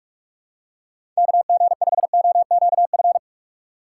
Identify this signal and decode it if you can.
Morse code, heard as a keyed tone, about 44 wpm, 700 Hz. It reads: KG5OQF
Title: USA Call Signs - 6 Characters - 6x Character Spacing - 45wpm